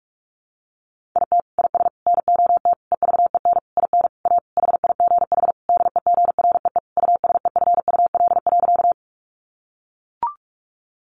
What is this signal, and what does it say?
Morse code, heard as a keyed tone, about 44 wpm, 700 Hz. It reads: IT IS NOT EVEN IN A HIGH DEGREE USEFUL.
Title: Sentences from Top 1500 Words 45wpm